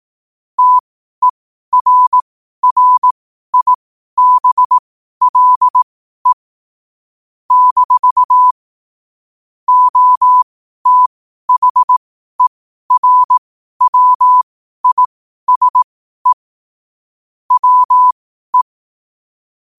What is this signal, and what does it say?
Morse code, heard as a keyed tone, about 18 wpm, spaced as for 12 wpm, 1000 Hz. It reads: TERRIBLE - OTHERWISE WE
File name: qotdfiles / 12wpm